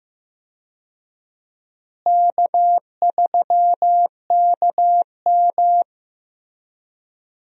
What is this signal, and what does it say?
Morse code, heard as a keyed tone, about 16 wpm, 700 Hz. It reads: K3KM
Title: USA Call Signs - 4 Characters - Repeated 3x - 15wpm